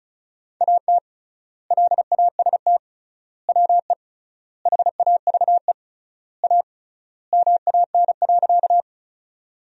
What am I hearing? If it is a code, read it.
Morse code, heard as a keyed tone, about 36 wpm, 700 Hz. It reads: AT LAST WE HAVE A MAN.
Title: 3x Word-Spacing - Sentences from Top 300 Words - 35wpm